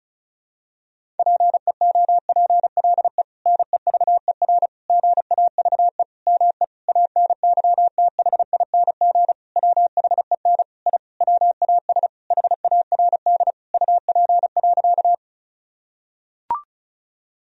Morse code, heard as a keyed tone, about 34 wpm, 700 Hz. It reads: PEOPLE NEVER GAVE ME ANYTHING WHEN I WAS HARD UP.
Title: Sentences from Top 1000 Words - Review Part 1 - 35wpm